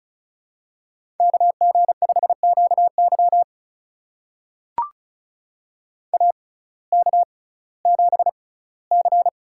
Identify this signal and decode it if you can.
Morse code, heard as a keyed tone, about 36 wpm, spaced as for 19 wpm, 700 Hz. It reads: KG5QY AK7C
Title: USA Call Signs - 6x Character Spacing - 35wpm